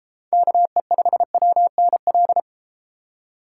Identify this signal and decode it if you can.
Morse code, heard as a keyed tone, about 32 wpm, 700 Hz. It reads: KE5WNL